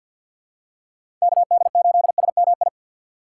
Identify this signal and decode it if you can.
Morse code, heard as a keyed tone, about 52 wpm, 700 Hz. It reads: KD8SMI
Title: USA Call Signs - 6 Characters - 3x Character Spacing - 50wpm